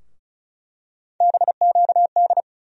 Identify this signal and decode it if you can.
Morse code, heard as a keyed tone, about 36 wpm, 700 Hz. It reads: BQD
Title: USA Call Sign Postfixes - 3 Letters 35wpm